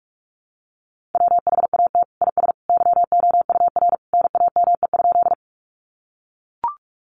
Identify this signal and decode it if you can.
Morse code, heard as a keyed tone, about 44 wpm, 700 Hz. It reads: WHAT IS YOUR NAME?
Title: Sentences from Top 200 Words 45wpm